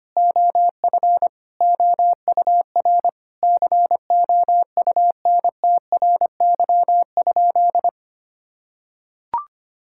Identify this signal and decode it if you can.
Morse code, heard as a keyed tone, about 25 wpm, 700 Hz. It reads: OF OUR COUNTRY?
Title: Sentences from Top 1000 Words - Review Part 5 - 25wpm